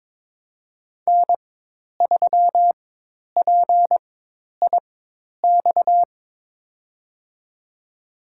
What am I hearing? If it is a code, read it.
Morse code, heard as a keyed tone, about 22 wpm, spaced as for 10 wpm, 700 Hz. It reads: N3PIX
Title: USA Call Signs - 5 Characters - 4x Character Spacing - 22wpm